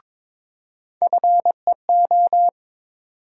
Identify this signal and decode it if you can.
Morse code, heard as a keyed tone, about 21 wpm, 700 Hz. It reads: FEO
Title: USA Call Sign Postfixes - 3 Letters 22wpm